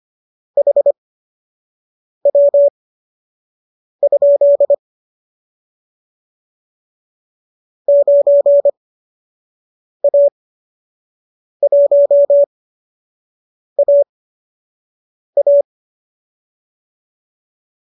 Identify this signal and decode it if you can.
Morse code, heard as a keyed tone, about 25 wpm, spaced as for 6 wpm, 570 Hz. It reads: HW? 9A1AA
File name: qso012 06